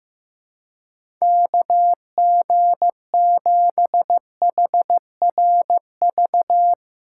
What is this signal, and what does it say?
Morse code, heard as a keyed tone, about 15 wpm, 700 Hz. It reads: KG7HRV
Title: USA Call Signs - 6 Characters - 2x Character Spacing - 15wpm